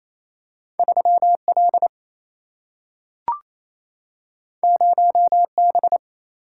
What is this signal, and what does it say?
Morse code, heard as a keyed tone, about 28 wpm, 700 Hz. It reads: 3L 0B
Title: Single-Letter Single-Number 28wpm